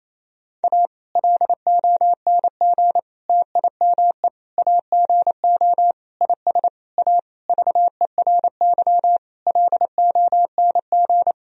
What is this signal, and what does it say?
Morse code, heard as a keyed tone, about 28 wpm, 700 Hz. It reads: A LONG TIME AGO IS A VERY LONG
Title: Sentences from Top 1000 Words - Review Part 3 - 28wpm